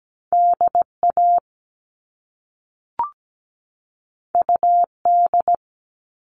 Morse code, heard as a keyed tone, about 17 wpm, 700 Hz. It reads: DA UD
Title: All Cut Numbers - 2-Digit 17wpm